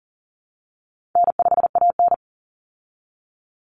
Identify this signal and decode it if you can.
Morse code, heard as a keyed tone, about 40 wpm, 700 Hz. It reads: N5AN
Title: Call Signs Repeated 3x - 40wpm